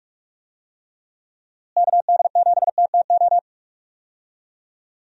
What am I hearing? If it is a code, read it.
Morse code, heard as a keyed tone, about 42 wpm, 700 Hz. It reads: KD7TTO
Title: USA Call Signs - 6 Characters - Repeated 3x - 45wpm